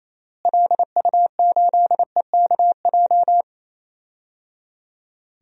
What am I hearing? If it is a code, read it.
Morse code, heard as a keyed tone, about 28 wpm, 700 Hz. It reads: LU8EKJ
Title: DXCC-100 Call Signs - 6 Characters - Repeated 3x - 28wpm